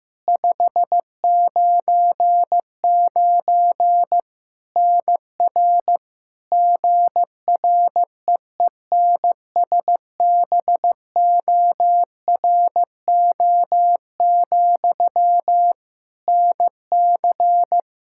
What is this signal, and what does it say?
Morse code, heard as a keyed tone, about 15 wpm, 700 Hz. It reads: 599 NR GREENSBORO, NC
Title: Practice QSO - Exchange 1 - Beginning - 15wpm